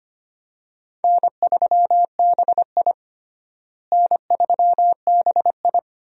Morse code, heard as a keyed tone, about 25 wpm, 700 Hz. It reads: N3BI N3BI
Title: USA Call Signs - 4 Characters - Repeated 2x - 25wpm